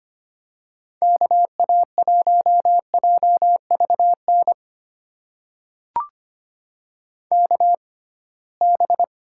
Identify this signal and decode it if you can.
Morse code, heard as a keyed tone, about 25 wpm, spaced as for 19 wpm, 700 Hz. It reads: KA1JVN KB
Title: USA Call Signs - 6 Characters - 6x Character Spacing - 25wpm